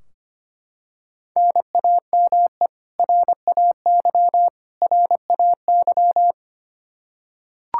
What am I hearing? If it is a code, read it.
Morse code, heard as a keyed tone, about 25 wpm, 700 Hz. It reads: NAME RAY RAY
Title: Practice QSO - Exchange 1 - Beginning - 25wpm